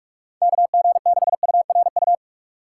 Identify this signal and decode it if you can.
Morse code, heard as a keyed tone, about 46 wpm, 700 Hz. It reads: KG6URU
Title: USA Call Signs - 2x Character Spacing - 45wpm